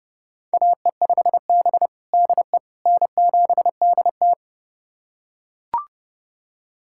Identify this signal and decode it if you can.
Morse code, heard as a keyed tone, about 30 wpm, 700 Hz. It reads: AE5B DE N7DT
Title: USA Callsign DE Callsign - 4 Characters - 30wpm